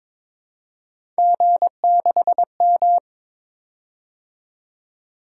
Morse code, heard as a keyed tone, about 22 wpm, 700 Hz. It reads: G6M
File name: practice / 22wpm / Three Letter-Number Combinations - 22wpm